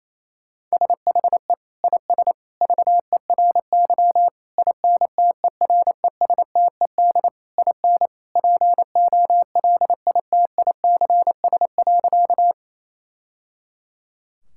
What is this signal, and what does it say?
Morse code, heard as a keyed tone, about 28 wpm, 700 Hz. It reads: SHE IS VERY INTERESTED IN POLITICS.
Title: Sentences from Top 1000 Words - Review Part 6 - 28wpm